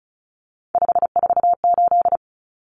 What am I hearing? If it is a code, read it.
Morse code, heard as a keyed tone, about 34 wpm, 700 Hz. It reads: 548
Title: Numbers - 3-Digit 35wpm